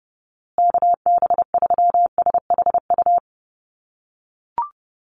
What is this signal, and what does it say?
Morse code, heard as a keyed tone, about 30 wpm, 700 Hz. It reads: KB3SHU
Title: USA Call Signs Repeated 3x 30wpm